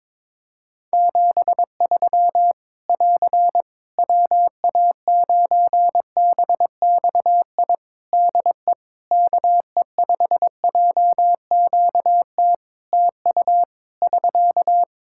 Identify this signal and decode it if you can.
Morse code, heard as a keyed tone, about 23 wpm, 700 Hz. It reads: 73 <AR> WA9BXI DE KE5JQT TU <SK>
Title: Practice QSO - Exchange 4 - Ending - 22wpm